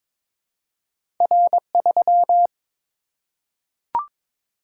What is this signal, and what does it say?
Morse code, heard as a keyed tone, about 22 wpm, 700 Hz. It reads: R3
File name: practice / 22wpm / Two Letter-Number Combinations - 22wpm